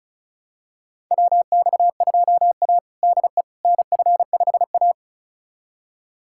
Morse code, heard as a keyed tone, about 36 wpm, 700 Hz. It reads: WX2A DE NF5A
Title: USA Callsign DE Callsign - 4 Characters - 35wpm